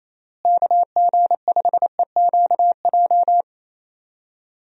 Morse code, heard as a keyed tone, about 28 wpm, 700 Hz. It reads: KG5EQJ
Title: USA Call Signs - 6 Characters - Repeated 3x - 28wpm